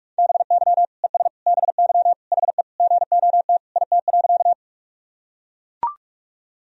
Morse code, heard as a keyed tone, about 41 wpm, 700 Hz. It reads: BY ES BY HE GOT IT.
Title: Sentences from Top 300 Words and Abbreviations 45wpm